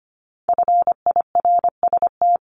Morse code, heard as a keyed tone, about 25 wpm, 700 Hz. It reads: FIRST
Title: Multi-letter Phonemes 25wpm